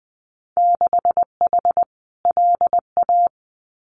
Morse code, heard as a keyed tone, about 20 wpm, 700 Hz. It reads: 6H LA
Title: Winter Field Day Exchanges Encore 20wpm